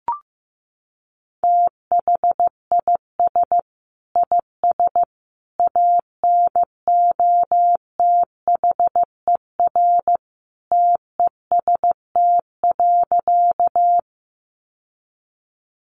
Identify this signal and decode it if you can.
Morse code, heard as a keyed tone, about 15 wpm, 700 Hz. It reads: THIS IS ANOTHER TEST.
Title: Sentences from Top 600 Words 15wpm